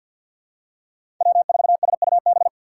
Morse code, heard as a keyed tone, about 49 wpm, 700 Hz. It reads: W4SUB
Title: USA Call Signs - 6x Character Spacing - 50wpm